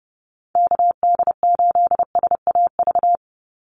Morse code, heard as a keyed tone, about 28 wpm, 700 Hz. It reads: KD8SAV
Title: Call Signs - 30wpm